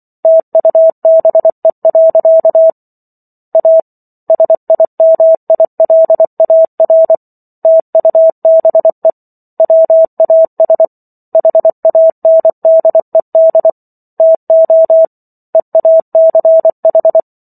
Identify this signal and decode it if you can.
Morse code, heard as a keyed tone, about 24 wpm, 650 Hz. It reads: TUBE. A SIMILAR TUBE WAS HANDED TO EACH